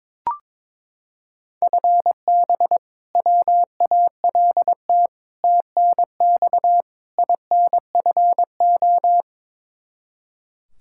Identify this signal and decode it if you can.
Morse code, heard as a keyed tone, about 22 wpm, 700 Hz. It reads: FB WALT TNX INFO
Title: Practice QSO - Exchange 3 - More Info - 22wpm